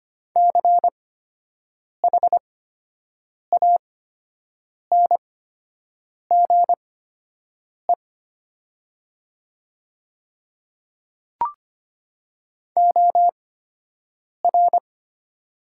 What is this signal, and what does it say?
Morse code, heard as a keyed tone, about 25 wpm, spaced as for 7 wpm, 700 Hz. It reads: CHANGE OR
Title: 8x Character Spacing - Top 300 Words - 25wpm